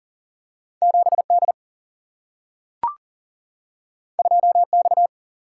Morse code, heard as a keyed tone, about 42 wpm, 700 Hz. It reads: 7 D 2 X
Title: Single-Letter Single-Number 40wpm